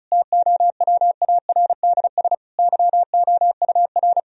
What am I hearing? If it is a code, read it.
Morse code, heard as a keyed tone, about 34 wpm, 700 Hz. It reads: TOWARDS YOUR
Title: Sentences from Top 1000 Words - Review Part 2 - 35wpm